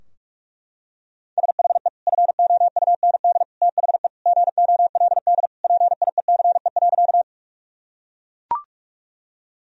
Morse code, heard as a keyed tone, about 42 wpm, 700 Hz. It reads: SHE FOUND THE GOLD PIECE.
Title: Sentences from Top 1000 Words - Review Part 3 - 45wpm